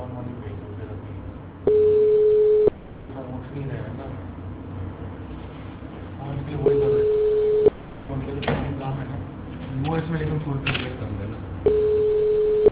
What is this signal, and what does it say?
Ringback tone (Europe, 425 Hz), with speech.